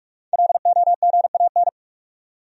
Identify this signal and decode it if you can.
Morse code, heard as a keyed tone, about 46 wpm, 700 Hz. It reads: LOGAN
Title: USA Common Names - 45wpm